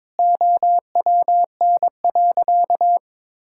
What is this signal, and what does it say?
Morse code, heard as a keyed tone, about 22 wpm, 700 Hz. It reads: OWN.
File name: Sentences with 7-Word Phrases - 22wpm